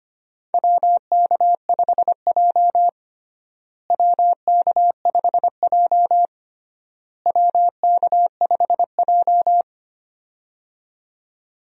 Morse code, heard as a keyed tone, about 25 wpm, 700 Hz. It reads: WK5J WK5J WK5J
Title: USA Call Signs - 4 Characters - Repeated 3x - 25wpm